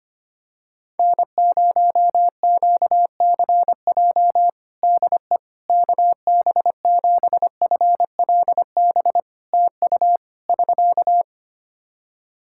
Morse code, heard as a keyed tone, about 25 wpm, 700 Hz. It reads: N0QCJ DE KB7FLB TU <SK>
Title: Practice QSO - Exchange 4 - Ending - 25wpm